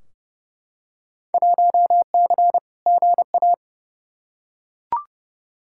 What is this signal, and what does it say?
Morse code, heard as a keyed tone, about 29 wpm, 700 Hz. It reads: 1C GA
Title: Field Day Exchanges Encore - 30wpm